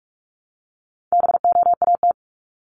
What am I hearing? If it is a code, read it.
Morse code, heard as a keyed tone, about 44 wpm, 700 Hz. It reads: BOAT
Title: Top 500 Words 45wpm